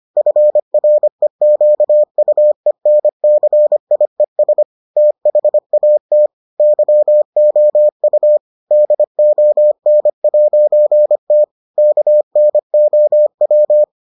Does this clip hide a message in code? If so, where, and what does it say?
Yes: Morse code from 0.2 to 13.8 s, heard as a keyed tone, about 25 wpm, 600 Hz. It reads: FREQUENCIES THAT YOU DON'T KNOW